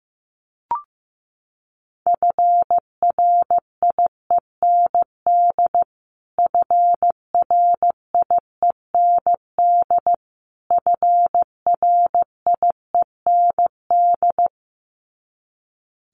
Morse code, heard as a keyed tone, about 15 wpm, 700 Hz. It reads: FRIEND FRIEND FRIEND
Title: Top 500 Words Repeated 15wpm